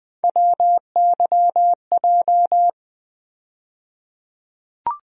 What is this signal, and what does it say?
Morse code, heard as a keyed tone, about 20 wpm, 700 Hz. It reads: WYJ
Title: USA Call Sign Postfixes - 3 Letters 20wpm